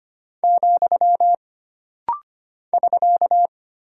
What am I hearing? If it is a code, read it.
Morse code, heard as a keyed tone, about 25 wpm, 700 Hz. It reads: , <SK>
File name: Single Punctuation-Prosign - Rapid-Fire - 25wpm